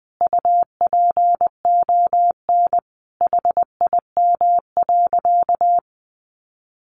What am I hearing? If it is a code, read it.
Morse code, heard as a keyed tone, about 20 wpm, 700 Hz. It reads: UPON HIM.
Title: Sentences from Top 1200 Words 20wpm